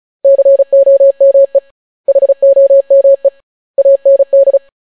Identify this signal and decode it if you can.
Morse code, heard as a keyed tone, about 36 wpm, 550 Hz. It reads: COME HOME AND